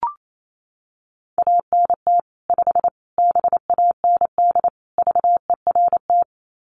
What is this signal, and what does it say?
Morse code, heard as a keyed tone, about 28 wpm, 700 Hz. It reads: ANT 5 BAND VERT
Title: Common QSO Phrases 28wpm